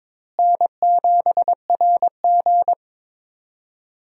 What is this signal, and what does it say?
Morse code, heard as a keyed tone, about 22 wpm, 700 Hz. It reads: N7RG
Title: USA Call Signs - 4 Characters - Repeated 2x - 22wpm